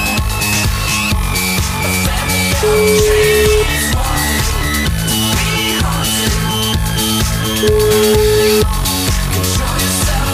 Ringback tone (Europe, 425 Hz), with noise.